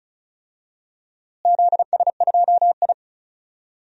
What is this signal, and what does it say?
Morse code, heard as a keyed tone, about 34 wpm, 700 Hz. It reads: ZS2I
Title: DXCC-100 Call Signs - 4 Characters - Repeated 3x - 35wpm